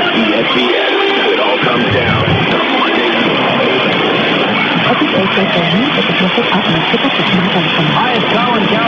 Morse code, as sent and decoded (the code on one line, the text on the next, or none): none